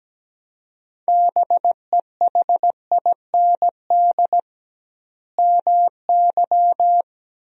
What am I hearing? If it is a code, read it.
Morse code, heard as a keyed tone, about 17 wpm, 700 Hz. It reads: BEHIND MY
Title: 2x Word-Spacing - Sentences from Top 500 Words - 17wpm